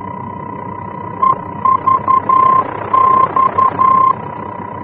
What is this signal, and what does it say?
Morse code, heard as a keyed tone, about 11 wpm, 1050 Hz, with noise. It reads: EVX
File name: france / mx / EVX-F-VOR